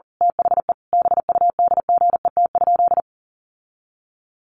Morse code, heard as a keyed tone, about 40 wpm, 700 Hz. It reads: THE BUDGET?
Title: Sentences from Top 1200 Words 40wpm